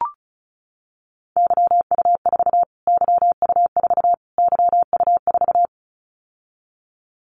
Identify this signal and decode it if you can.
Morse code, heard as a keyed tone, about 34 wpm, 700 Hz. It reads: YU4 YU4 YU4
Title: Callsign Prefixes for Top 25 DXCC - Repeated 3x - 35wpm